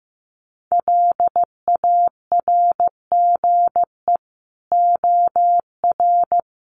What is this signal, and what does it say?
Morse code, heard as a keyed tone, about 15 wpm, 700 Hz. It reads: LARGE OR
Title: Sentences from Top 200 Words 15wpm